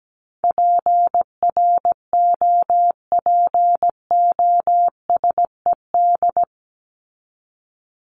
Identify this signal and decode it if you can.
Morse code, heard as a keyed tone, about 17 wpm, 700 Hz. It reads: PROPOSED
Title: ED Suffix - Top 100 Words with Base Words 17wpm